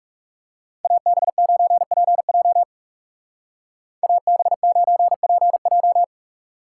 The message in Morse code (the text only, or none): AB9PJ AB9PJ